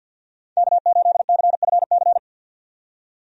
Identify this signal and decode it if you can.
Morse code, heard as a keyed tone, about 46 wpm, 700 Hz. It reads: K8CFC
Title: USA Call Signs - 5 Characters - 50wpm